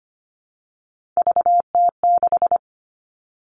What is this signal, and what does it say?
Morse code, heard as a keyed tone, about 25 wpm, 700 Hz. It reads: VT6
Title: All Cut Numbers - 3-Digit 25wpm